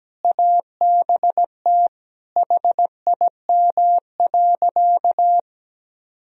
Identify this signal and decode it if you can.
Morse code, heard as a keyed tone, about 17 wpm, 700 Hz. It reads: ABT HIM.
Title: Sentences from Top 200 Words and Abbreviations 17wpm